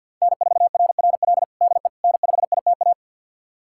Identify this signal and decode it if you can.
Morse code, heard as a keyed tone, about 52 wpm, 700 Hz. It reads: N4RRL DE N5ITA
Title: USA Callsign DE Callsign - 5 Characters - 50wpm